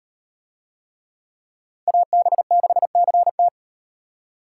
Morse code, heard as a keyed tone, about 38 wpm, 700 Hz. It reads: AB6CT